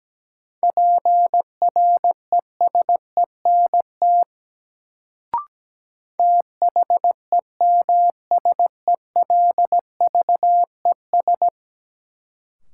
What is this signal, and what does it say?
Morse code, heard as a keyed tone, about 17 wpm, 700 Hz. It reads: PRESENT THEMSELVES
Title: Sentences from Top 800 Words - New Word Drill - 17wpm